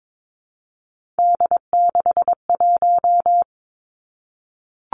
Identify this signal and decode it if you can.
Morse code, heard as a keyed tone, about 22 wpm, 700 Hz. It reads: D61
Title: Three Challenging Letter-Number Combinations 22wpm